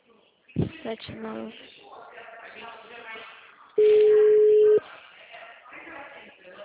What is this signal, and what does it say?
Ringback tone (Europe, 425 Hz), with speech.
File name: out- -2016- - - .63